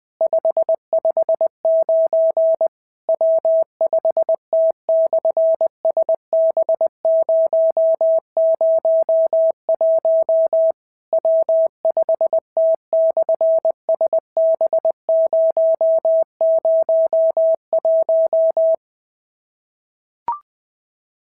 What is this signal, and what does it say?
Morse code, heard as a keyed tone, about 20 wpm, 650 Hz. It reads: 559 W5T/SB001 W5T/SB001
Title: SOTA QSO Practice 20wpm